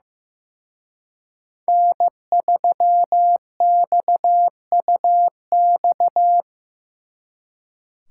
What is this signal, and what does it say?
Morse code, heard as a keyed tone, about 18 wpm, 700 Hz. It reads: N3XUX
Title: Speed Racing - USA 1x3 Call Signs 15wpm